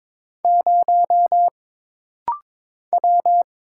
Morse code, heard as a keyed tone, about 22 wpm, 700 Hz. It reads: 0 W
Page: This rapid-fire timing is geared to push your ICR (Instant Character Recognition) skills.